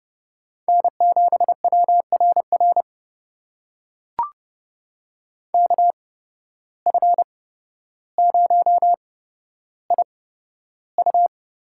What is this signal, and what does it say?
Morse code, heard as a keyed tone, about 30 wpm, spaced as for 13 wpm, 700 Hz. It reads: N7WRR KF0IU
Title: USA Call Signs - 8x Character Spacing - 30wpm